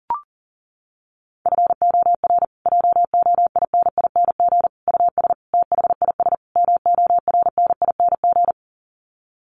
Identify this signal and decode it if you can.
Morse code, heard as a keyed tone, about 40 wpm, 700 Hz. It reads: FOR JOINING US THIS MORNING
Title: Sets of 5 Words Encore 40wpm